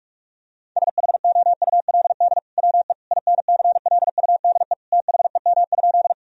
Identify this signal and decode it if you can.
Morse code, heard as a keyed tone, about 46 wpm, 700 Hz. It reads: SHOULD WE INCLUDE THEM?
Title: Sentences from Top 1000 Words - Review Part 3 - 45wpm